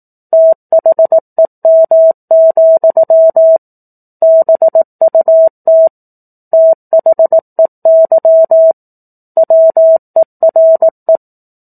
Morse code, heard as a keyed tone, about 18 wpm, 650 Hz. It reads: THEM, BUT THEY WERE
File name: 04-Chapter-04 18 WPM